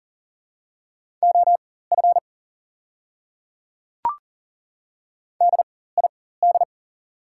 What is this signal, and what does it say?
Morse code, heard as a keyed tone, about 40 wpm, spaced as for 19 wpm, 690 Hz. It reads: OF DID
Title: 4x Character Spacing - Top 100 Words - 40wpm